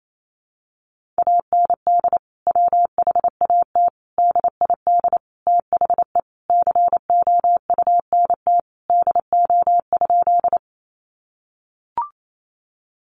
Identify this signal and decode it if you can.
Morse code, heard as a keyed tone, about 28 wpm, 700 Hz. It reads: AND WHAT DID THE COUNT DO?
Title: Sentences from Top 1500 Words 28wpm